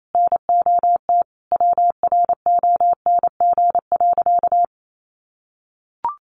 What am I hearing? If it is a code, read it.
Morse code, heard as a keyed tone, about 28 wpm, 700 Hz. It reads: NOT WRONG.
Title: Sentences from Top 900 Words 28wpm